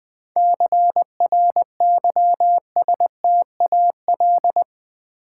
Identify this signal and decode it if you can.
Morse code, heard as a keyed tone, about 20 wpm, 700 Hz. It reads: CRYSTAL
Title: 7 Letter Words - First 3 Letters Given - 20wpm